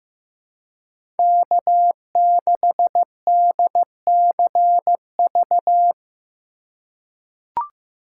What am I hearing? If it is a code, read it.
Morse code, heard as a keyed tone, about 15 wpm, 700 Hz. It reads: K6DCV
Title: USA Call Signs - 5 Characters - 4x Character Spacing - 15wpm